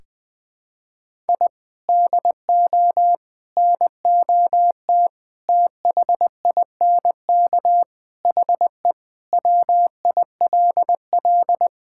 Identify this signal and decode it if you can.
Morse code, heard as a keyed tone, about 20 wpm, 700 Hz. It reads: I DO NOT THINK HE WILL
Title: Sentences with 5-Word Phrases - Spoken Beginning - 20wpm